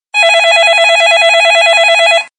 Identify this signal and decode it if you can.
There is no code.